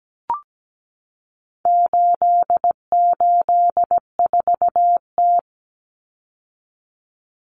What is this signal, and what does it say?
Morse code, heard as a keyed tone, about 17 wpm, 700 Hz. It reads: 884T
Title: Common Cut Numbers - 4-Digit 17wpm